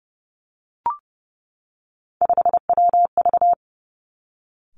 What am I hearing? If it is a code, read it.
Morse code, heard as a keyed tone, about 30 wpm, 700 Hz. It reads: 5WV
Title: Three Challenging Letter-Number Combinations 30wpm